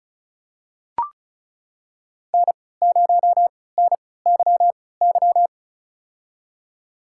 Morse code, heard as a keyed tone, about 34 wpm, spaced as for 20 wpm, 700 Hz. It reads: N0NYY
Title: USA Call Signs - 5 Characters - 3x Character Spacing - 35wpm